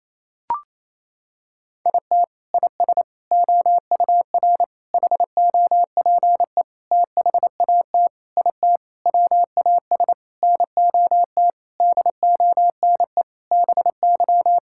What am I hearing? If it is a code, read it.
Morse code, heard as a keyed tone, about 28 wpm, 700 Hz. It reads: IT IS OUR HOPE THAT IT WAS NOT DONE BY
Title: Sentences from Top 1000 Words - Review Part 5 - 28wpm